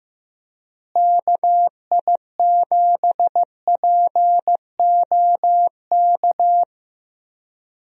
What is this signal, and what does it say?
Morse code, heard as a keyed tone, about 15 wpm, 700 Hz. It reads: KI7POK
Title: USA Call Signs - 6 Characters - Repeated 3x - 15wpm